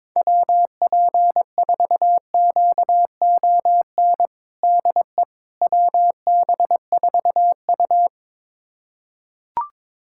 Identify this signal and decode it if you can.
Morse code, heard as a keyed tone, about 22 wpm, 700 Hz. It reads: WP4QON DE WB4U
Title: USA Callsign DE Callsign 22wpm